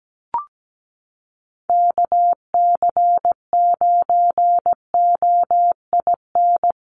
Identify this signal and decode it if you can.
Morse code, heard as a keyed tone, about 17 wpm, 700 Hz. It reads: KC9OIN
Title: USA Call Signs Repeated 2x 17wpm